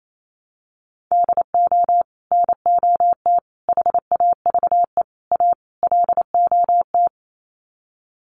Morse code, heard as a keyed tone, about 28 wpm, 700 Hz. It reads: DO NOT HAVE A LOT
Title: Sets of 5 Words Encore 28wpm